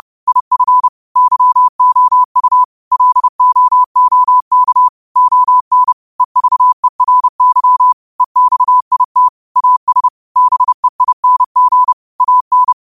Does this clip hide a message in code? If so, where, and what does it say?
Yes: Morse code from 0.3 to 12.7 s, heard as a keyed tone, about 31 wpm, 1000 Hz. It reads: IF YOU LOOK ON EVERY EXIT AS BEING AN